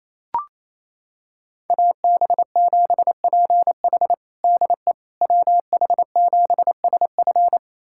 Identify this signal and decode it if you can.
Morse code, heard as a keyed tone, about 28 wpm, 700 Hz. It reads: AB7PH DE WH7SF
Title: USA Callsign DE Callsign - 5 Characters - 28wpm